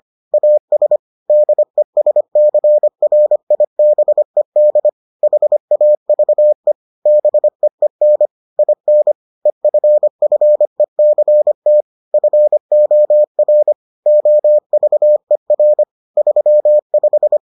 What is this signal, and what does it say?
Morse code, heard as a keyed tone, about 25 wpm, 600 Hz. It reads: AS DESCRIBED HAVE BEEN IN EFFECT FOR OVER 35